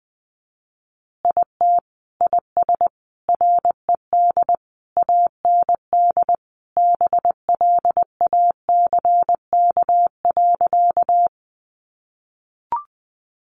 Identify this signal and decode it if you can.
Morse code, heard as a keyed tone, about 20 wpm, 700 Hz. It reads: IT IS RED AND BLACK.
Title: Additional Easy Phrases 20wpm